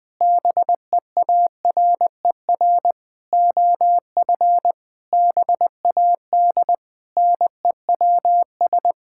Morse code, heard as a keyed tone, about 21 wpm, 700 Hz. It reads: BEARER OF BAD NEWS
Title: Sentences with 9-Word Phrases - 20wpm